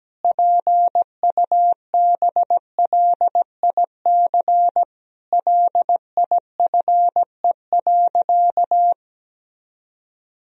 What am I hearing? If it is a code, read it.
Morse code, heard as a keyed tone, about 17 wpm, 700 Hz. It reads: PUBLIC LIFE.
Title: Sentences from Top 1000 Words - Review Part 3 - 17wpm